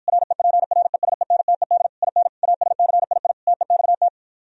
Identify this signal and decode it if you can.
Morse code, heard as a keyed tone, about 52 wpm, 700 Hz. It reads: REPRESENTED IN ASCII TEXT